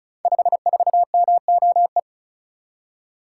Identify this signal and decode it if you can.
Morse code, heard as a keyed tone, about 36 wpm, 700 Hz. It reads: 54MOE
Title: Code Groups - 5 Letter:Numbers - 35wpm